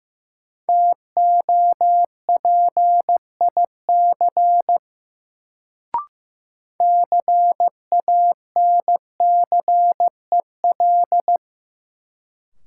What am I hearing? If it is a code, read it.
Morse code, heard as a keyed tone, about 16 wpm, 700 Hz. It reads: TOPIC CANCEL
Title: Sentences from Top 1200 Words - New Word Drill - 15wpm